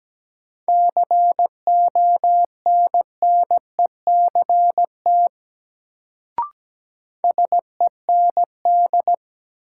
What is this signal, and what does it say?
Morse code, heard as a keyed tone, about 17 wpm, 700 Hz. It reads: CONNECT SEND
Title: Sentences from Top 1500 Words - New Word Drill - 17wpm